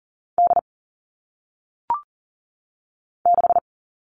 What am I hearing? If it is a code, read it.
Morse code, heard as a keyed tone, about 40 wpm, 700 Hz. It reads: D 6
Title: All Cut Numbers - 1-Digit 40wpm